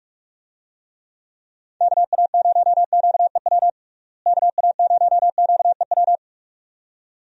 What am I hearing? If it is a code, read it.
Morse code, heard as a keyed tone, about 42 wpm, 700 Hz. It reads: KA0QEW KA0QEW
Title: USA Call Signs - 6 Characters - Repeated 2x - 45wpm